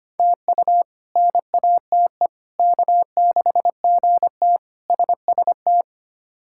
Morse code, heard as a keyed tone, about 25 wpm, 700 Hz. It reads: TU NATE K6GT SST
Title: K1USN SST QSO Practice - 25wpm